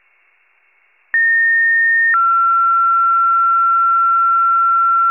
Two-tone page, heard as a one-second tone, then a three-second tone, about 1850 then 1360 Hz. It is heard from 1.1 to 5.1 s.
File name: pager / 33 WCFM / wcalert